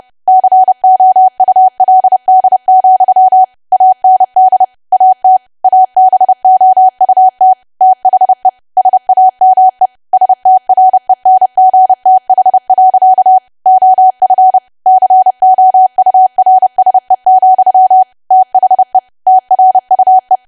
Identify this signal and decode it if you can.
Morse code, heard as a keyed tone, about 30 wpm, 750 Hz. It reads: COULD, AND AT ABOUT THE SAME STRENGTH. OF COURSE, THE TRUE
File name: cwa / W1AW-Level3 / 30WPM